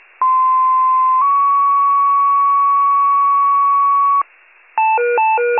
Two-tone page, heard as a one-second tone, then a three-second tone, about 1040 then 1120 Hz. It is heard from 0.2 to 4.2 s.
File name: wcalert